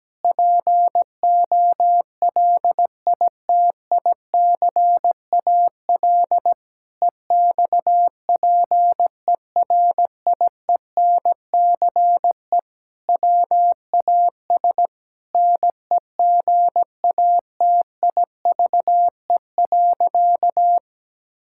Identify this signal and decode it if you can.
Morse code, heard as a keyed tone, about 17 wpm, 700 Hz. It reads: POLITICAL EXPERIENCE WAS NEGATIVE.
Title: Sentences from Top 1000 Words - Review Part 6 - 17wpm